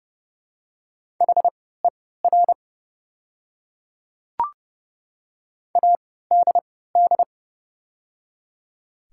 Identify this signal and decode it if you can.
Morse code, heard as a keyed tone, about 30 wpm, spaced as for 17 wpm, 690 Hz. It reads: HER ADD
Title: 3x Character Spacing - Top 200 Words - 30wpm